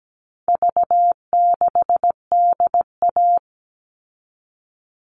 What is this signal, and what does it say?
Morse code, heard as a keyed tone, about 17 wpm, 700 Hz. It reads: V6DA
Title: All Cut Numbers - 4-Digit 17wpm